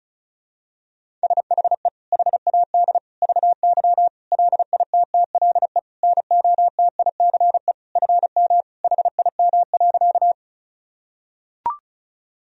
Morse code, heard as a keyed tone, about 34 wpm, 700 Hz. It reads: SHE HAD VY LITTLE NOTICE FM HIM.
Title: Sentences from Top 400 Words and Abbreviations 35wpm